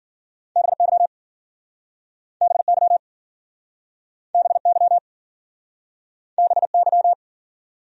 Morse code, heard as a keyed tone, about 48 wpm, 700 Hz. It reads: BY BY BY BY
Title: Speed-Racing Top 100 Words 40wpm